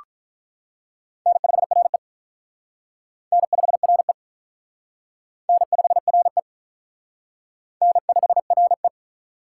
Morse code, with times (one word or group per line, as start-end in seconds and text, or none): 1.3-2.0 N5RE
3.3-4.1 N5RE
5.5-6.4 N5RE
7.8-8.9 N5RE